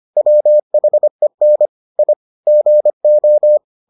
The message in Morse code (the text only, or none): WHEN I GO